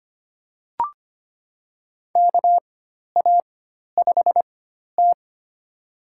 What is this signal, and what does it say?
Morse code, heard as a keyed tone, about 25 wpm, spaced as for 12 wpm, 700 Hz. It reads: KA5T
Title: USA Call Signs - 4 Characters - 4x Character Spacing - 25wpm